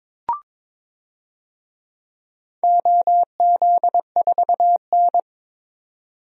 Morse code, heard as a keyed tone, about 22 wpm, 700 Hz. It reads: OZ4N
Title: DXCC-100 Call Signs - 4 Characters - 22wpm